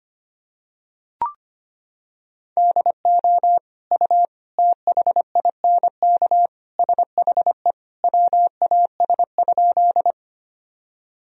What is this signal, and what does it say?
Morse code, heard as a keyed tone, about 25 wpm, 700 Hz. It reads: DO U THINK SHE WAS?
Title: Sentences from Top 200 Words and Abbreviations 25wpm